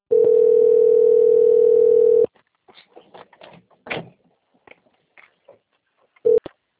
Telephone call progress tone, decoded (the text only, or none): ringback tone